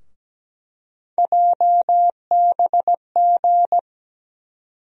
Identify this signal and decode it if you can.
Morse code, heard as a keyed tone, about 17 wpm, 700 Hz. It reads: JBG
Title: USA Call Sign Postfixes - 3 Letters 17wpm